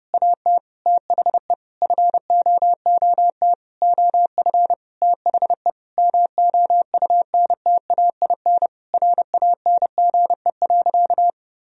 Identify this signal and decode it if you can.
Morse code, heard as a keyed tone, about 32 wpm, 700 Hz. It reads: AT THE FOOT OF THE MOUNTAIN RANGE.
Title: Sentences from Top 1000 Words - Review Part 4 - 30wpm